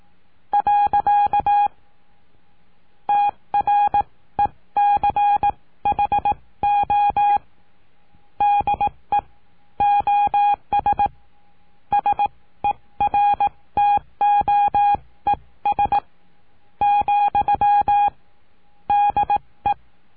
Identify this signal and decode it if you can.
Morse code, heard as a keyed tone, about 18 wpm, spaced as for 14 wpm, 820 Hz. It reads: . TRECHO DE OS SERTOES, DE
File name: cursos / CURSO DE TELEGRAFIA / Telegrafia - Trilha 19